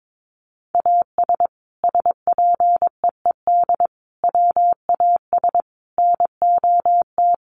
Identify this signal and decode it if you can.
Morse code, heard as a keyed tone, about 22 wpm, 700 Hz. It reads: AS SPEED WAS NOT
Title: Sentences from Top 800 Words 22wpm